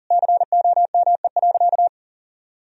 Morse code, heard as a keyed tone, about 40 wpm, 700 Hz. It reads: COME.
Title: Sentences from Top 1000 Words - Review Part 2 - 40wpm